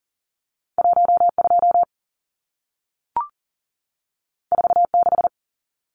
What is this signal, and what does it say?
Morse code, heard as a keyed tone, about 40 wpm, 700 Hz. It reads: 12 46
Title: Numbers - 2-Digit 40wpm